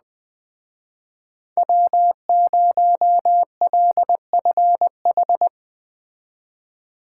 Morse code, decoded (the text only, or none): W0LFH